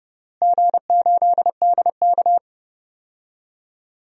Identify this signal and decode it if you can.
Morse code, heard as a keyed tone, about 30 wpm, 700 Hz. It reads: G8DK
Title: DXCC-100 Call Signs - 4 Characters - Repeated 3x - 30wpm